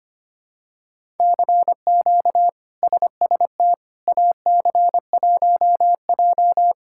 Morse code, heard as a keyed tone, about 25 wpm, 700 Hz. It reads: CQ SST AC1J
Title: K1USN SST QSO Practice - 25wpm